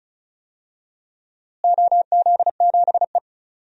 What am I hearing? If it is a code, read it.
Morse code, heard as a keyed tone, about 36 wpm, 690 Hz. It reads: OZ7E
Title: DXCC-100 Call Signs - 4 Characters - Repeated 3x - 35wpm